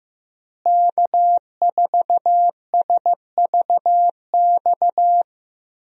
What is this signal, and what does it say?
Morse code, heard as a keyed tone, about 15 wpm, 700 Hz. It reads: K4SVX
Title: USA Call Signs - 5 Characters - Repeated 3x - 15wpm